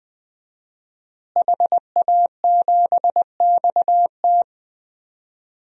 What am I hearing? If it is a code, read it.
Morse code, heard as a keyed tone, about 20 wpm, 700 Hz. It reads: HA7XT
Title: DXCC-100 Call Signs - 5 Characters - Repeated 2x - 20wpm